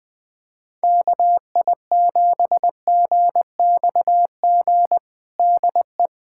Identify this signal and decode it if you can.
Morse code, heard as a keyed tone, about 20 wpm, 700 Hz. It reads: KI7GXG DE
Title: USA Callsign DE Callsign 20wpm